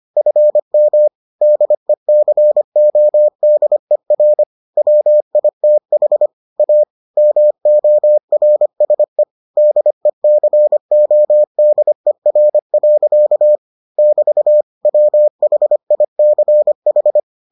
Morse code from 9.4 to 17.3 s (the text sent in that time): DECODER. <BT> WHICH